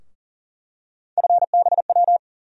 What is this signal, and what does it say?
Morse code, heard as a keyed tone, about 40 wpm, 700 Hz. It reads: FBW
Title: USA Call Sign Postfixes - 3 Letters 40wpm